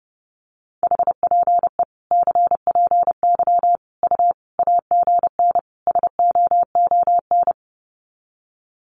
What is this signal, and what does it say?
Morse code, heard as a keyed tone, about 30 wpm, 700 Hz. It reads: HPE CPY U AGN SOON
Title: Common QSO Phrases 30wpm